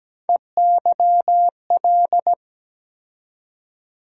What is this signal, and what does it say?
Morse code, heard as a keyed tone, about 17 wpm, 700 Hz. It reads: EYL
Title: USA Call Sign Postfixes - 3 Letters 17wpm